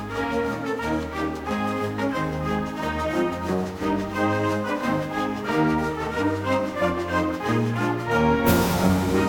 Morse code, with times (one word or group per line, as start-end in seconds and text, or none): none